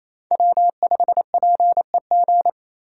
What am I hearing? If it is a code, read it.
Morse code, heard as a keyed tone, about 28 wpm, 700 Hz. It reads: W5PEG
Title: USA Call Signs - 3x Character Spacing - 28wpm